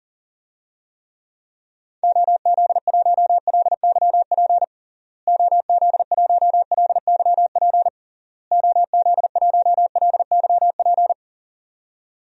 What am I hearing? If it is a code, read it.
Morse code, heard as a keyed tone, about 40 wpm, 700 Hz. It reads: OZ1LYP OZ1LYP OZ1LYP
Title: DXCC-100 Call Signs - 6 Characters - Repeated 3x - 40wpm